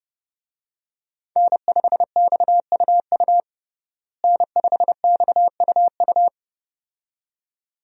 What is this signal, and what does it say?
Morse code, heard as a keyed tone, about 30 wpm, 700 Hz. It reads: N5XUU N5XUU
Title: USA Call Signs - 5 Characters - Repeated 2x - 30wpm